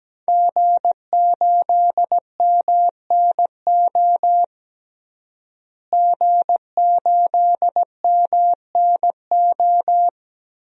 Morse code, heard as a keyed tone, about 17 wpm, 700 Hz. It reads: G8MNO G8MNO
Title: DXCC-100 Call Signs - 5 Characters - Repeated 2x - 17wpm